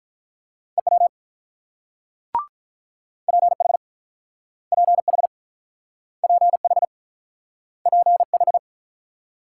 Morse code, heard as a keyed tone, about 48 wpm, 700 Hz. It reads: EW PH PH PH PH